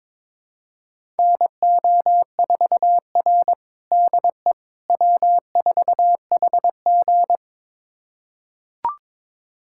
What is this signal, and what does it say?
Morse code, heard as a keyed tone, about 22 wpm, 700 Hz. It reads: NO4R DE W4HG
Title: USA Callsign DE Callsign - 4 Characters - 22wpm